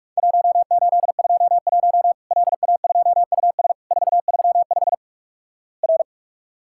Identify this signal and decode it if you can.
Morse code, heard as a keyed tone, about 48 wpm, 700 Hz. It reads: 1821 LA2US 435 R
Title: Worked All Europe DX Contest - QTC Exchange - 45wpm